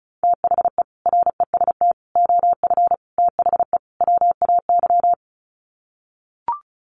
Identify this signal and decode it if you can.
Morse code, heard as a keyed tone, about 34 wpm, 700 Hz. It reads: THE REST OF THE WAY
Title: Sets of 5 Words Encore 35wpm